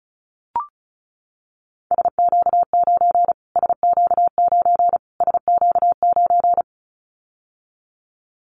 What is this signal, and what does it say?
Morse code, heard as a keyed tone, about 34 wpm, 700 Hz. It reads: SQ9 SQ9 SQ9
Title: Callsign Prefixes for Top 25 DXCC - Repeated 3x - 35wpm